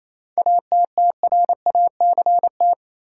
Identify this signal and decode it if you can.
Morse code, heard as a keyed tone, about 27 wpm, 700 Hz. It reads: ATTRACT
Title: Seven Letter Words - 28wpm